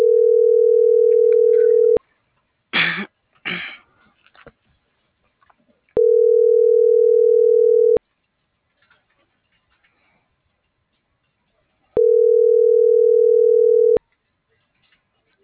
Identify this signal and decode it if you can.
Ringback tone (North America), 440 + 480 Hz.
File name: CENTRAL MAYAB / out lizbeth.mayab - .24669